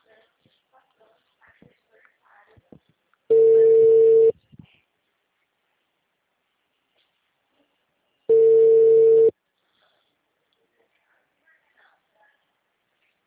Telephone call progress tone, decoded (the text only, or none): ringback tone